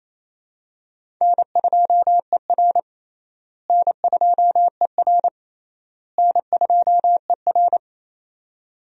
Morse code, heard as a keyed tone, about 27 wpm, 700 Hz. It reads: N2ER N2ER N2ER
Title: USA Call Signs - 4 Characters - Repeated 3x - 28wpm